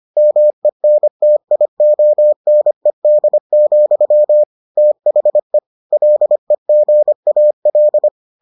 Morse code, heard as a keyed tone, about 25 wpm, 600 Hz. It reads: MENTIONED, THE LEGAL